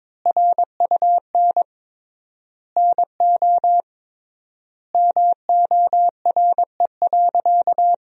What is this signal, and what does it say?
Morse code, heard as a keyed tone, about 22 wpm, 700 Hz. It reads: RUN NO MORE.
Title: 3x Word-Spacing - Sentences from Top 300 Words - 22wpm